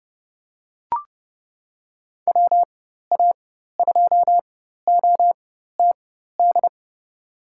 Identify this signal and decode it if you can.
Morse code, heard as a keyed tone, about 30 wpm, spaced as for 14 wpm, 700 Hz. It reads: WA2OTD
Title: USA Call Signs - 4x Character Spacing - 30wpm